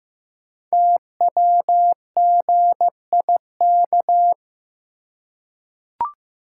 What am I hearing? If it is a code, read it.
Morse code, heard as a keyed tone, about 15 wpm, 700 Hz. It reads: TWGIK
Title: Code Groups - 5 Letters - 15wpm